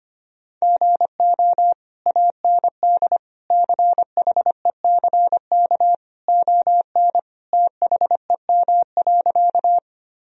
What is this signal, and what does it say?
Morse code, heard as a keyed tone, about 25 wpm, 700 Hz. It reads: GO AND CHECK ON THEM.
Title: Sentences from Top 1000 Words - Review Part 3 - 25wpm